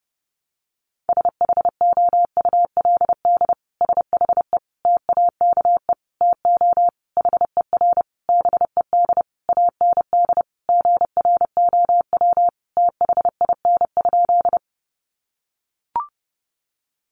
Morse code, heard as a keyed tone, about 30 wpm, 700 Hz. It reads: SHOULD SHE TAKE TO HER BED AND GROW THIN?
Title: Sentences from Top 1200 Words 30wpm